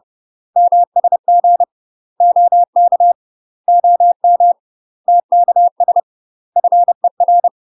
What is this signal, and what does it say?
Morse code, heard as a keyed tone, about 30 wpm, 700 Hz. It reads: MSG OK OM TKS FER
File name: UK MASTER QSOs TRAINING / 30WPM / UK MASTER QSOs 0002